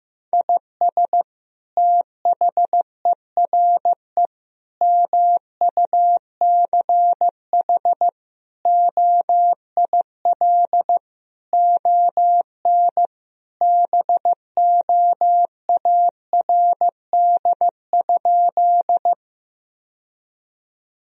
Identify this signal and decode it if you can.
Morse code, heard as a keyed tone, about 15 wpm, 700 Hz. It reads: IS THERE MUCH OIL ON BOARD?
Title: Sentences from Top 1000 Words - Review Part 4 - 15wpm